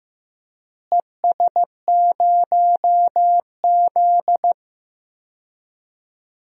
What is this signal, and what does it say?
Morse code, heard as a keyed tone, about 15 wpm, 700 Hz. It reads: ES0Z
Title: DXCC-100 Call Signs - 4 Characters - Repeated 2x - 15wpm